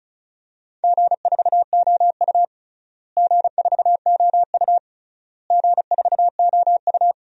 Morse code, heard as a keyed tone, about 36 wpm, 690 Hz. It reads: G4OU G4OU G4OU
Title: DXCC-100 Call Signs - 4 Characters - Repeated 3x - 35wpm